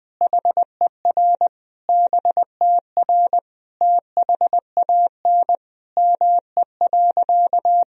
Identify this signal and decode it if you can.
Morse code, heard as a keyed tone, about 20 wpm, 700 Hz. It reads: HER BTR THAN ME.
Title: Sentences from Top 400 Words and Abbreviations 20wpm